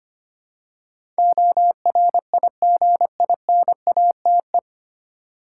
Morse code, heard as a keyed tone, about 25 wpm, 700 Hz. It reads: ORIGINATE
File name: Nine Letter Words - 25wpm